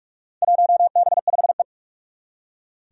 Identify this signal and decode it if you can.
Morse code, heard as a keyed tone, about 45 wpm, 700 Hz. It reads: 1B5E
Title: Four Challenging Letter-Number Combinations 45wpm